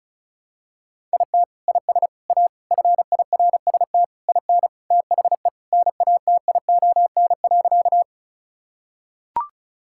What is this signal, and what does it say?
Morse code, heard as a keyed tone, about 36 wpm, 700 Hz. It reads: IT IS A FIRST IN THE NATION.
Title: Sentences from Top 1000 Words - Review Part 6 - 35wpm